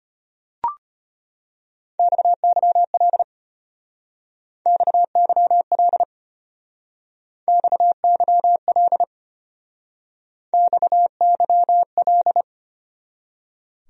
Morse code, heard as a keyed tone, about 32 wpm, 700 Hz. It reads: XYL XYL XYL XYL